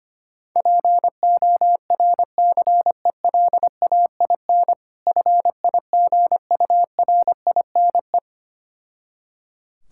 Morse code, heard as a keyed tone, about 25 wpm, 690 Hz. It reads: PORCELAIN FIGURINE
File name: practice / 25wpm / Difficult Words with Hints - 25wpm